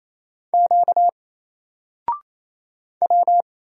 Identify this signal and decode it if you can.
Morse code, heard as a keyed tone, about 27 wpm, 700 Hz. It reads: Q W
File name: Single-Letter-Number-28wpm